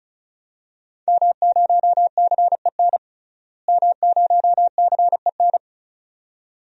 Morse code, heard as a keyed tone, about 36 wpm, 700 Hz. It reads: M0CEN M0CEN
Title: DXCC-100 Call Signs - 5 Characters - Repeated 2x - 35wpm